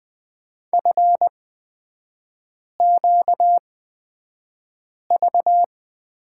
Morse code, heard as a keyed tone, about 20 wpm, spaced as for 5 wpm, 700 Hz. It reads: FQV
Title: 255 Lesson 41e - Period - Three Characters 20wpm